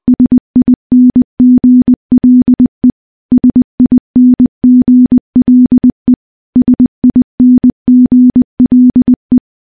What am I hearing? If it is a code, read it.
Morse code, heard as a keyed tone, about 20 wpm, 250 Hz. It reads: SINGLE SINGLE SINGLE
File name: morse-training-20WPM